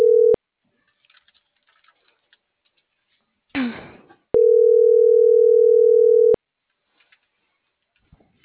Ringback tone (North America), 440 + 480 Hz.
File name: CENTRAL NMP / out melanie.diaznmp - .37691